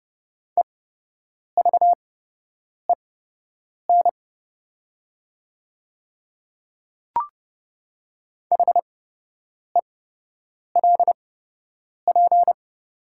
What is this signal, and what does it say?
Morse code, heard as a keyed tone, about 29 wpm, spaced as for 8 wpm, 690 Hz. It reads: EVEN HELP
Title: 8x Character Spacing - Top 200 Words - 30wpm